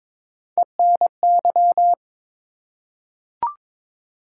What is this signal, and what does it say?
Morse code, heard as a keyed tone, about 22 wpm, 700 Hz. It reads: ENY
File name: ARRL Sections v2 - 22wpm